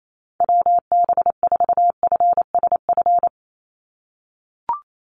Morse code, heard as a keyed tone, about 28 wpm, 700 Hz. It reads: WB4FSF
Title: Call Signs - 28wpm